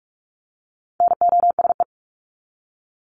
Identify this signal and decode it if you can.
Morse code, heard as a keyed tone, about 44 wpm, 700 Hz. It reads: NOSE
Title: Top 1501-1750 Words 45wpm